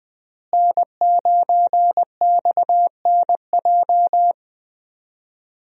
Morse code, heard as a keyed tone, about 20 wpm, 700 Hz. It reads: N9XNJ
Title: USA Call Signs - 5 Characters - Repeated 3x - 20wpm